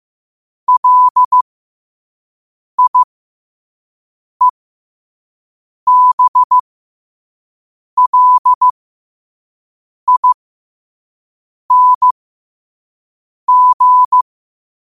Morse code, heard as a keyed tone, about 15 wpm, spaced as for 5 wpm, 1000 Hz. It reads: LIEBLING
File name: qotdfiles / 5wpm